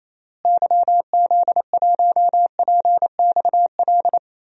Morse code, heard as a keyed tone, about 28 wpm, 700 Hz. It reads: YZ1PXL
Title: DXCC-100 Call Signs - 6 Characters - Repeated 3x - 28wpm